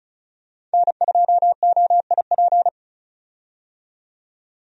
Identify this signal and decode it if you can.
Morse code, heard as a keyed tone, about 36 wpm, 700 Hz. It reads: N2OIP
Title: USA Call Signs - 5 Characters - 35wpm